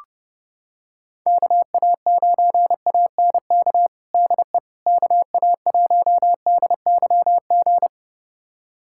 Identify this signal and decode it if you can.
Morse code, heard as a keyed tone, about 30 wpm, 700 Hz. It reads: KA9ANK DE KA1DYG
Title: USA Callsign DE Callsign - 6 Characters - 30wpm